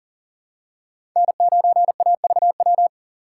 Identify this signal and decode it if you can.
Morse code, heard as a keyed tone, about 40 wpm, 700 Hz. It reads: N9AVW
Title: USA Call Signs - 5 Characters - 2x Character Spacing - 40wpm